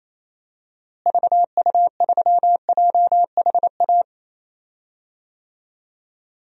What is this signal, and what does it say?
Morse code, heard as a keyed tone, about 28 wpm, 700 Hz. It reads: VU3JHA
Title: DXCC-100 Call Signs - 6 Characters - Repeated 2x - 28wpm